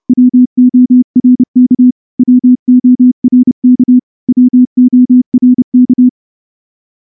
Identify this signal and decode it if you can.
Morse code, heard as a keyed tone, about 28 wpm, 250 Hz. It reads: WORK WORK WORK
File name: morse-training-30WPM